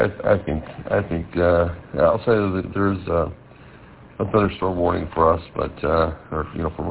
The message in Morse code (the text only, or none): none